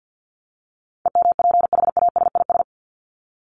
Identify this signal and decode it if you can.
Morse code, heard as a keyed tone, about 50 wpm, 700 Hz. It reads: EMPHASIS
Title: Top 1251-1500 Words 50wpm